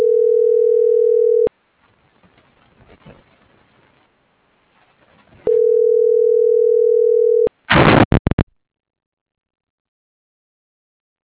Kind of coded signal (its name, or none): telephone call progress tone